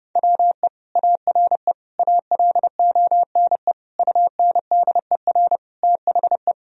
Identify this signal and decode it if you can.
Morse code, heard as a keyed tone, about 32 wpm, 700 Hz. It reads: WE ARE ALONE UNDER THE
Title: Sentences from Top 1000 Words - Review Part 1 - 30wpm